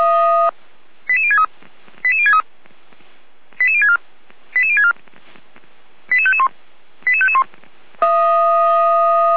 Five-tone selective call (ZVEI), heard as quick runs of five tones, tones of about 70 ms. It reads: 88063 88063 88064 88064 88551 88551
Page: SDR/ZVEI